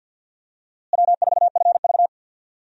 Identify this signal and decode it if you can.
Morse code, heard as a keyed tone, about 52 wpm, 690 Hz. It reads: W4FV
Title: USA Call Signs - 4 Characters - Repeated 2x - 50wpm